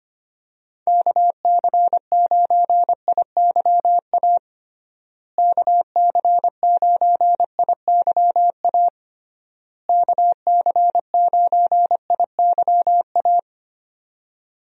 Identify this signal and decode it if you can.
Morse code, heard as a keyed tone, about 25 wpm, 700 Hz. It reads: KC9IYA KC9IYA KC9IYA
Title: USA Call Signs - 6 Characters - Repeated 3x - 25wpm